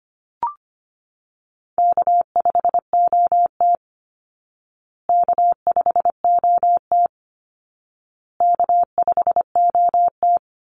Morse code, heard as a keyed tone, about 25 wpm, 700 Hz. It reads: K5OT K5OT K5OT
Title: Call Signs Repeated 2x - 25wpm